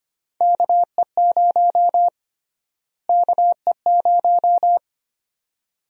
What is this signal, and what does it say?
Morse code, heard as a keyed tone, about 25 wpm, 700 Hz. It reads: KE0 KE0
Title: USA Call Sign Prefixes - Repeated 2x - 25wpm